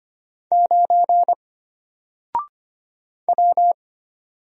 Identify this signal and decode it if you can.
Morse code, heard as a keyed tone, about 25 wpm, 700 Hz. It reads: 9 W
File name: Single-Letter-Number-25wpm